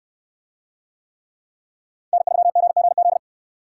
Morse code, heard as a keyed tone, about 63 wpm, 700 Hz. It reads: N3GGZ